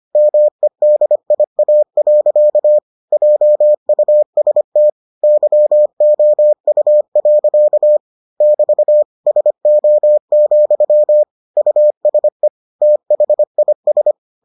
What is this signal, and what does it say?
Morse code, heard as a keyed tone, about 25 wpm, 600 Hz. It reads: MEDIA. JUST YOU. <BT> SO, USE THIS